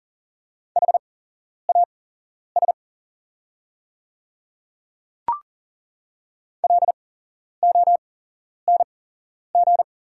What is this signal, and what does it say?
Morse code, heard as a keyed tone, about 40 wpm, spaced as for 11 wpm, 700 Hz. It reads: HAS LONG
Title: 8x Character Spacing - Top 100 Words - 40wpm